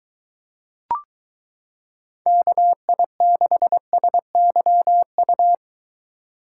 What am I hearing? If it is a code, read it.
Morse code, heard as a keyed tone, about 23 wpm, 700 Hz. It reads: KI6SYU